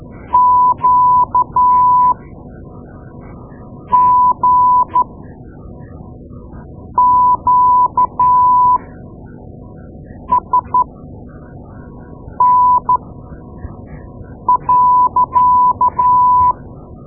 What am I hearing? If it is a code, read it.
Morse code, heard as a keyed tone, about 12 wpm, spaced as for 4 wpm, 1000 Hz. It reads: QGQSN.
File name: Morse Lessons / Morse Lessons / Tape-2A